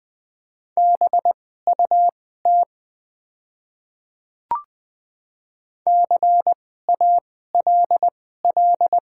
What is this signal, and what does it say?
Morse code, heard as a keyed tone, about 20 wpm, spaced as for 14 wpm, 700 Hz. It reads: BUT CALL
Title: 2x Character Spacing - Top 100 Words - 20wpm